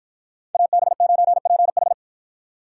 Morse code, heard as a keyed tone, about 53 wpm, 700 Hz. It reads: AB9PH